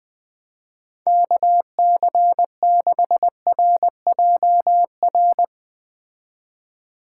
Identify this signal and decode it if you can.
Morse code, heard as a keyed tone, about 21 wpm, 700 Hz. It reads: KC6RJR